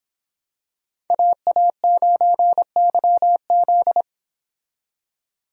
Morse code, heard as a keyed tone, about 26 wpm, 700 Hz. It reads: AA9YZ
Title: Speed Racing - USA Call Signs 22wpm